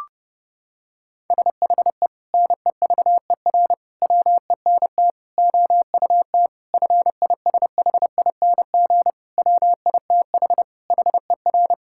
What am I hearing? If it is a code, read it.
Morse code, heard as a keyed tone, about 30 wpm, 700 Hz. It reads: SHE NEVER WENT OUT FISHING WITH HER
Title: Sentences from Top 1000 Words - Review Part 5 - 30wpm